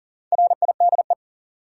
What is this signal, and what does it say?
Morse code, heard as a keyed tone, about 40 wpm, 700 Hz. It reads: RIDE
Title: Sentences from Top 1000 Words - Review Part 6 - New Word Drill - 40wpm